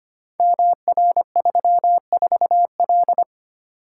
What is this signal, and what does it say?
Morse code, heard as a keyed tone, about 25 wpm, 700 Hz. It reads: MR34L
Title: Code Groups - 5 Letter:Numbers - 25wpm